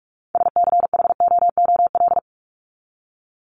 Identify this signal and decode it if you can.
Morse code, heard as a keyed tone, about 44 wpm, 700 Hz. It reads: SCHOOL
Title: Top 500 Words 45wpm